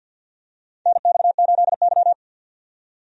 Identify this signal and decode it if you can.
Morse code, heard as a keyed tone, about 51 wpm, 700 Hz. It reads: NX8Y
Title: USA Call Signs - 4 Characters - 50wpm